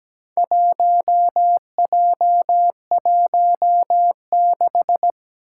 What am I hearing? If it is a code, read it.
Morse code, heard as a keyed tone, about 17 wpm, 700 Hz. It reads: 1J16
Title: Four Challenging Letter-Number Combinations 17wpm